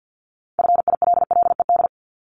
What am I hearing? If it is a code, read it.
Morse code, heard as a keyed tone, about 50 wpm, 700 Hz. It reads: FILLED
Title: Multi-letter Phonemes 50wpm